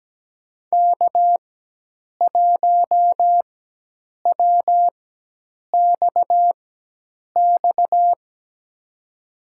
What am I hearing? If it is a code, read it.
Morse code, heard as a keyed tone, about 17 wpm, spaced as for 8 wpm, 700 Hz. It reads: K1WXX
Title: USA Call Signs - 5 Characters - 4x Character Spacing - 17wpm